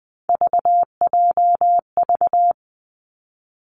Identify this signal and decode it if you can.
Morse code, heard as a keyed tone, about 20 wpm, 700 Hz. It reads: VJV
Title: Three Challenging Letter-Number Combinations 20wpm